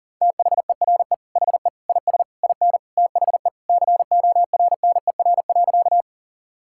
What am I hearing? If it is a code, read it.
Morse code, heard as a keyed tone, about 40 wpm, 700 Hz. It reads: THERE HE IS IN THE CORNER.
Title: Sentences from Top 1600 Words 40wpm